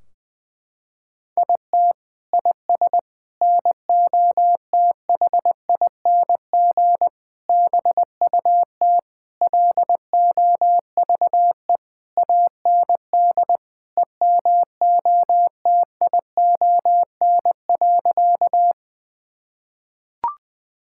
Morse code, heard as a keyed tone, about 20 wpm, 700 Hz. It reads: IT IS NOTHING BUT LOVE AND EMOTION.
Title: Sentences from Top 1600 Words 20wpm